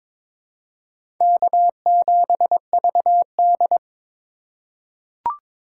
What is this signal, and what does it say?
Morse code, heard as a keyed tone, about 22 wpm, 700 Hz. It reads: K7VD
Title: USA Call Signs - 4 Characters - 8x Character Spacing - 22wpm